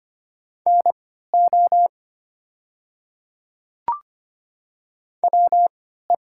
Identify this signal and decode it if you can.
Morse code, heard as a keyed tone, about 25 wpm, spaced as for 14 wpm, 700 Hz. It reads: NO WE
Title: 3x Character Spacing - Top 100 Words - 25wpm